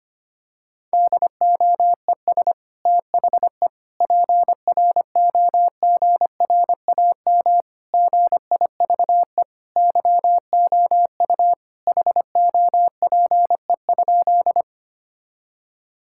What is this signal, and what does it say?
Morse code, heard as a keyed tone, about 25 wpm, 700 Hz. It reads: DOES THE PROGRAM GIVE YOU HOPE?
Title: Sentences from Top 1000 Words - Review Part 5 - 25wpm